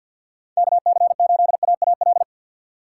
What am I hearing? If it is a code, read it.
Morse code, heard as a keyed tone, about 51 wpm, 700 Hz. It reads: KC8AAL
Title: USA Call Signs - 6 Characters - 6x Character Spacing - 50wpm